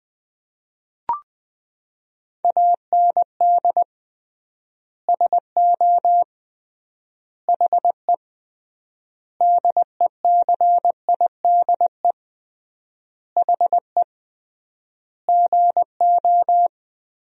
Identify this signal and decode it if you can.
Morse code, heard as a keyed tone, about 20 wpm, 700 Hz. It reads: AND SO HE DECIDE HE GO
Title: 3x Word-Spacing - Sentences from Top 500 Words - 20wpm